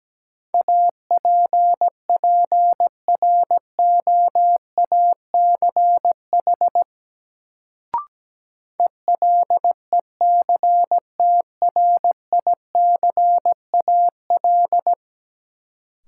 Morse code, heard as a keyed tone, about 17 wpm, 700 Hz. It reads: APPROACH ELECTRICAL
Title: Sentences from Top 1000 Words - Review Part 6 - New Word Drill - 17wpm